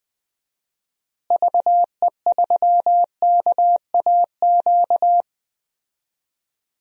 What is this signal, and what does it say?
Morse code, heard as a keyed tone, about 20 wpm, 700 Hz. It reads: VE3KAQ
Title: DXCC-100 Call Signs - 6 Characters - Repeated 2x - 20wpm